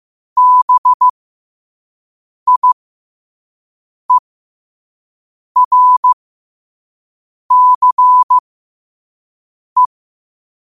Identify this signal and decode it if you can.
Morse code, heard as a keyed tone, about 15 wpm, spaced as for 5 wpm, 1000 Hz. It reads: BIERCE
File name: qotdfiles / 5wpm